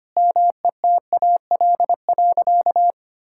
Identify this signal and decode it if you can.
Morse code, heard as a keyed tone, about 25 wpm, 700 Hz. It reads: METAL.
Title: Sentences from Top 1600 Words 25wpm